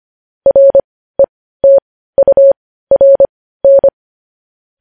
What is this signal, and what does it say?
Morse code, heard as a keyed tone, about 25 wpm, spaced as for 15 wpm, 550 Hz. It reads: RETURN
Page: news-headlines-in-morse-code-at-15-wpm